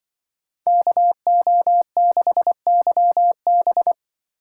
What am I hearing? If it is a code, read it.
Morse code, heard as a keyed tone, about 25 wpm, 700 Hz. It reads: KO6YB
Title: Speed Racing - USA 2x2 Call Signs 20wpm